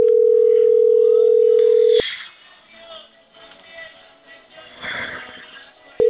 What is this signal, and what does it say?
Ringback tone (North America), 440 + 480 Hz.